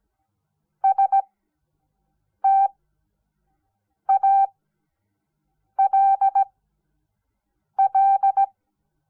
Morse code, heard as a keyed tone, about 17 wpm, spaced as for 6 wpm, 780 Hz. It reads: STALL
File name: Onlinelehrgang / morsen / DJ4UF Morsen06